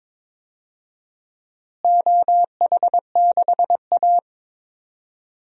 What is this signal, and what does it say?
Morse code, heard as a keyed tone, about 22 wpm, 700 Hz. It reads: OH6A
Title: DXCC-100 Call Signs - 4 Characters - Repeated 2x - 22wpm